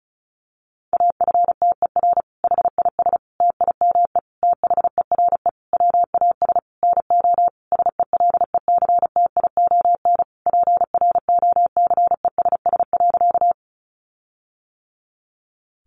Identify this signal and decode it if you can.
Morse code, heard as a keyed tone, about 34 wpm, 700 Hz. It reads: AFTER HIS TIME THERE WAS NO SELECTION PROCESS.
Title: Sentences from Top 1400 Words 35wpm